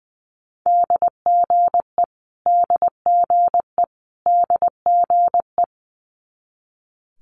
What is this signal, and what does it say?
Morse code, heard as a keyed tone, about 20 wpm, 700 Hz. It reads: DGE DGE DGE
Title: Multi-letter Phonemes Repeated 3x 20wpm